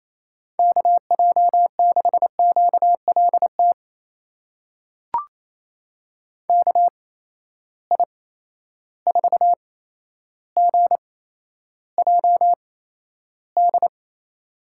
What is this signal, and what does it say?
Morse code, heard as a keyed tone, about 28 wpm, spaced as for 12 wpm, 700 Hz. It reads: KJ6QLT KI4GJD
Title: USA Call Signs - 6 Characters - 8x Character Spacing - 28wpm